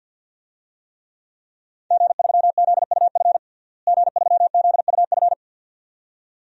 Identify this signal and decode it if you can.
Morse code, heard as a keyed tone, about 53 wpm, 700 Hz. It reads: G3ZUF G3ZUF
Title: DXCC-100 Call Signs - 5 Characters - Repeated 2x - 50wpm